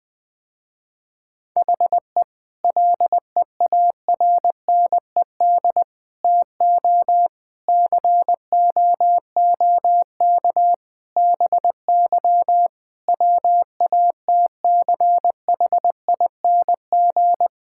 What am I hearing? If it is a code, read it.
Morse code, heard as a keyed tone, about 20 wpm, 700 Hz. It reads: HE LEARNED TO COOK BY WATCHING
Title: Difficult Sentences with Hints - 20wpm